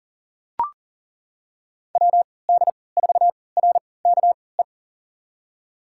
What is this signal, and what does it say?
Morse code, heard as a keyed tone, about 40 wpm, spaced as for 23 wpm, 700 Hz. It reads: WD4RKE
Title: USA Call Signs - 3x Character Spacing - 40wpm